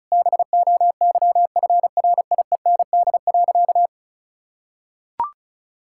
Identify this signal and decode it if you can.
Morse code, heard as a keyed tone, about 36 wpm, 700 Hz. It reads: BOYFRIEND.
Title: Sentences from Top 1000 Words - Review Part 6 - 35wpm